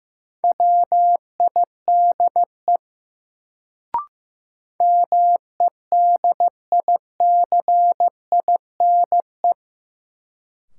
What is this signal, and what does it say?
Morse code, heard as a keyed tone, about 15 wpm, 700 Hz. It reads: WIDE MEDICINE
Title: Sentences from Top 900 Words - New Word Drill - 15wpm